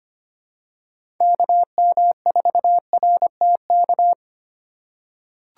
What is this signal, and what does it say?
Morse code, heard as a keyed tone, about 25 wpm, 700 Hz. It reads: KM4RTK
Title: USA Call Signs - 6 Characters - 2x Character Spacing - 25wpm